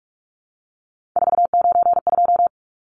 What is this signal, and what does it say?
Morse code, heard as a keyed tone, about 44 wpm, 700 Hz. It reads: 492
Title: Numbers - 3-Digit 45wpm